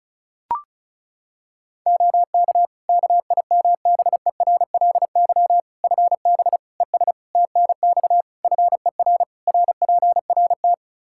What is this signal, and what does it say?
Morse code, heard as a keyed tone, about 36 wpm, 700 Hz. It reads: OK KIMBERLY FB ES TNX FER RPRT
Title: Practice QSO - Exchange 2 - Info - 35wpm